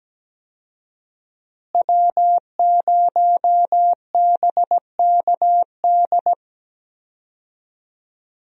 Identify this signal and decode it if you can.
Morse code, heard as a keyed tone, about 17 wpm, 700 Hz. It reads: W0BKD
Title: USA Call Signs - 5 Characters - Repeated 2x - 17wpm